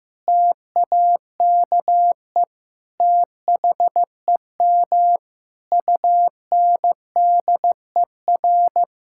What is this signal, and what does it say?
Morse code, heard as a keyed tone, about 16 wpm, 700 Hz. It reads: TAKE THEM UNDER
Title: Sentences from Top 1600 Words 15wpm